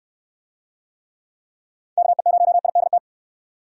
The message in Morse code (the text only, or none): KE0IDT